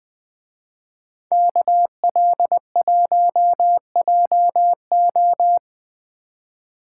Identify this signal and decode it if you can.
Morse code, heard as a keyed tone, about 19 wpm, 700 Hz. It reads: KL1JO